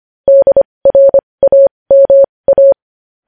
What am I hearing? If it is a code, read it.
Morse code, heard as a keyed tone, about 25 wpm, spaced as for 20 wpm, 550 Hz. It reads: DRAMA
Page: news-headlines-in-morse-code-at-20-wpm